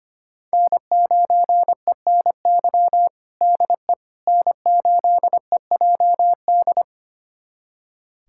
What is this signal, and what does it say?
Morse code, heard as a keyed tone, about 25 wpm, 700 Hz. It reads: N9ENY DE N8EJD
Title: USA Callsign DE Callsign - 5 Characters - 25wpm